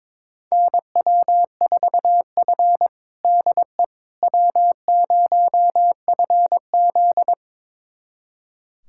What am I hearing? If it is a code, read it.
Morse code, heard as a keyed tone, about 22 wpm, 700 Hz. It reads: NW4F DE W0FZ
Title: USA Callsign DE Callsign - 4 Characters - 22wpm